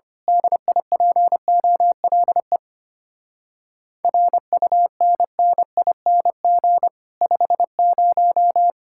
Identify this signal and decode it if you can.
Morse code, heard as a keyed tone, about 25 wpm, 700 Hz. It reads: DIPOLE RUNNING 50
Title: Speed-Racing Common QSO Phrases - 25wpm